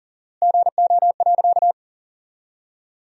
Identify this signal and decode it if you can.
Morse code, heard as a keyed tone, about 38 wpm, 700 Hz. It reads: GO.
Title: 4x Word-Spacing - Sentences from Top 400 Words - 40wpm